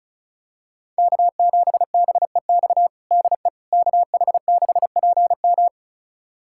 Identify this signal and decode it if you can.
Morse code, heard as a keyed tone, about 36 wpm, 700 Hz. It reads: K7BEX DE KH6PM
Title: USA Callsign DE Callsign - 5 Characters - 35wpm